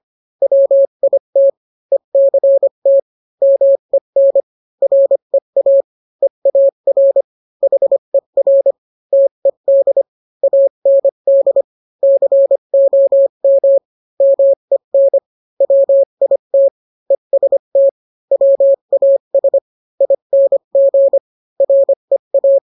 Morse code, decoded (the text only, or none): WIT ECT MEN REA EAR HER TED AND COM MEN WIT EST WAS ING REA